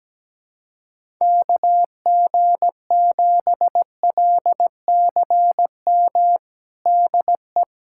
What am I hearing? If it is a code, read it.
Morse code, heard as a keyed tone, about 17 wpm, 700 Hz. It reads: KG7LCM DE
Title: USA Callsign DE Callsign - 6 Characters - 17wpm